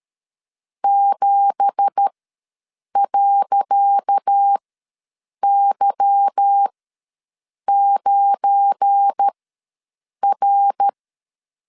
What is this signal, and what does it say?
Morse code, heard as a keyed tone, about 13 wpm, spaced as for 7 wpm, 800 Hz. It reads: 7.Y9R
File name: LEZIONI / 10 SIGNS / 10C DOT ws 12wpm 3548